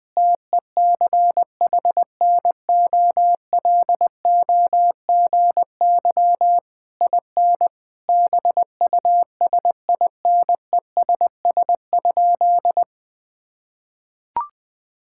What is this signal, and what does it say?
Morse code, heard as a keyed tone, about 20 wpm, 700 Hz. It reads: TECHNOLOGY IN BUSINESS?
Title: Sentences from Top 1000 Words - Review Part 5 - 20wpm